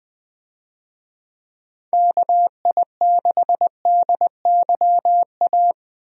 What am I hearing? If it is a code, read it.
Morse code, heard as a keyed tone, about 20 wpm, 700 Hz. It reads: KI6DYA
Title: USA Call Signs - 6 Characters - 20wpm